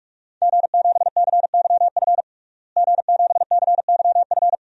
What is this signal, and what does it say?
Morse code, heard as a keyed tone, about 42 wpm, 700 Hz. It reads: G7CYF G7CYF
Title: DXCC-100 Call Signs - 5 Characters - Repeated 2x - 45wpm